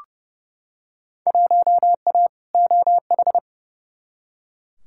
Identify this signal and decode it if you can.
Morse code, heard as a keyed tone, about 30 wpm, 700 Hz. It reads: 1A OH
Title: Field Day Exchanges - 30wpm